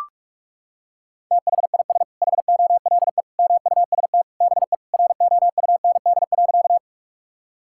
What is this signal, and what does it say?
Morse code, heard as a keyed tone, about 41 wpm, 700 Hz. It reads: THIS HOLE MUST BE ROUND.
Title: Sentences from Top 1600 Words 45wpm